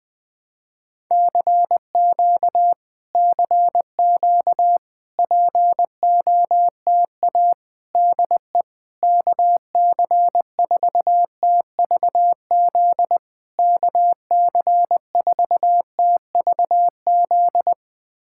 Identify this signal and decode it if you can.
Morse code, heard as a keyed tone, about 21 wpm, 700 Hz. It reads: CQ CQ POTA DE KC4TVZ KC4TVZ
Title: POTA QSO Practice - 20wpm